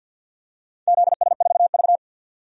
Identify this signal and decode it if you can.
Morse code, heard as a keyed tone, about 52 wpm, 700 Hz. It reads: ZS4V
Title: DXCC-100 Call Signs - 4 Characters - 50wpm